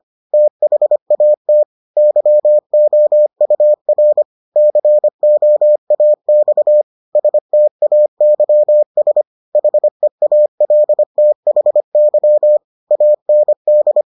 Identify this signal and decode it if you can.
Morse code, heard as a keyed tone, about 25 wpm, 600 Hz. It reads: THAT YOUR COAX STAYS HEALTHY AND